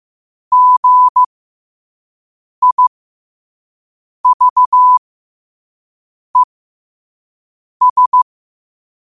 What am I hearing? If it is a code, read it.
Morse code, heard as a keyed tone, about 15 wpm, spaced as for 5 wpm, 1000 Hz. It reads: GIVES
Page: New quotes every day in morse code at 5 Words per minute.